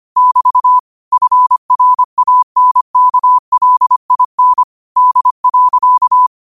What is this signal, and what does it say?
Morse code, heard as a keyed tone, about 25 wpm, 1000 Hz. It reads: <BT> FRANKLIN D.
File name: qotdfiles / 25wpm